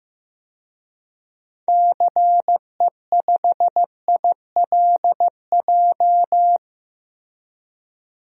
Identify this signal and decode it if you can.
Morse code, heard as a keyed tone, about 15 wpm, 700 Hz. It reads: CE5ILJ
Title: DXCC-100 Call Signs - 6 Characters - 15wpm